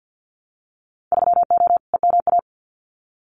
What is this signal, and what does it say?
Morse code, heard as a keyed tone, about 50 wpm, 700 Hz. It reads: 3O EMA
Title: Winter Field Day Exchanges 50wpm